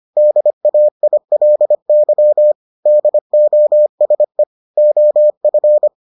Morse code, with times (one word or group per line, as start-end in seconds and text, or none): 0.2-2.5 DAILY
2.9-4.4 DOSE
4.8-5.9 OF